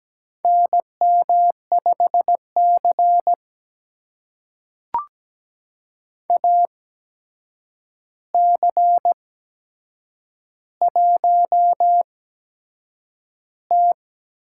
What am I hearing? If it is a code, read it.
Morse code, heard as a keyed tone, about 17 wpm, spaced as for 7 wpm, 700 Hz. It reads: NM5C AC1T
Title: USA Call Signs - 4 Characters - 8x Character Spacing - 17wpm